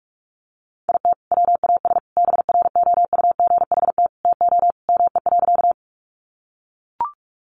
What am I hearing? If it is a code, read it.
Morse code, heard as a keyed tone, about 44 wpm, 700 Hz. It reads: IT WAS BROUGHT TO ME.
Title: Sentences from Top 500 Words 45wpm